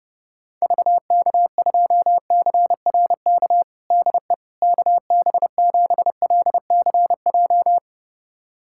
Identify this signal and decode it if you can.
Morse code, heard as a keyed tone, about 30 wpm, 700 Hz. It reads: VK2CRK DE KB7LCJ
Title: USA Callsign DE Callsign 30wpm